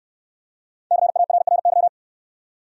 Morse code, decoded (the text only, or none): CARRY